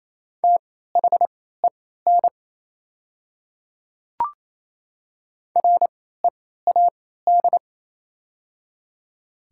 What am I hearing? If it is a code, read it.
Morse code, heard as a keyed tone, about 28 wpm, spaced as for 16 wpm, 700 Hz. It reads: THEN READ
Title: 3x Character Spacing - Top 200 Words - 28wpm